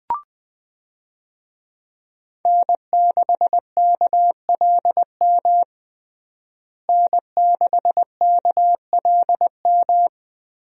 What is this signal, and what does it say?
Morse code, heard as a keyed tone, about 21 wpm, 700 Hz. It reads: N6KLM N6KLM
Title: USA Call Signs - 5 Characters - Repeated 2x - 20wpm